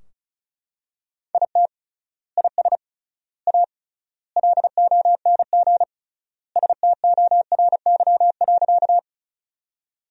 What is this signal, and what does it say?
Morse code, heard as a keyed tone, about 36 wpm, 700 Hz. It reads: IT IS A LONG STORY.
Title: 3x Word-Spacing - Sentences from Top 300 Words - 35wpm